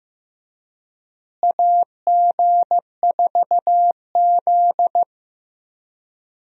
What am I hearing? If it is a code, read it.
Morse code, heard as a keyed tone, about 15 wpm, 700 Hz. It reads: AG4Z
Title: USA Call Signs - 4 Characters - Repeated 2x - 15wpm